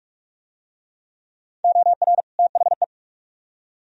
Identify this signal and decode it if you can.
Morse code, heard as a keyed tone, about 44 wpm, 700 Hz. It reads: OR THE